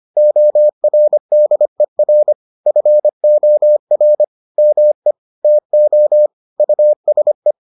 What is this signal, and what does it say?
Morse code, heard as a keyed tone, about 25 wpm, 600 Hz. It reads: ORDER FOR ME TO USE